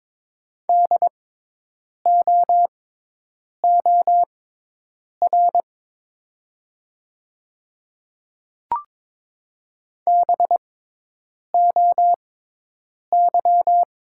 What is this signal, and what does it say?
Morse code, heard as a keyed tone, about 22 wpm, spaced as for 8 wpm, 700 Hz. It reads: DOOR BOY
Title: 6x Character Spacing - Top 500 Words - 22wpm